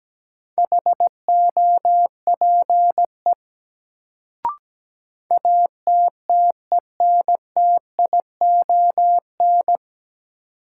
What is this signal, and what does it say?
Morse code, heard as a keyed tone, about 17 wpm, 700 Hz. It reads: HOPE ATTENTION
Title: Sentences from Top 1000 Words - New Word Drill - 17wpm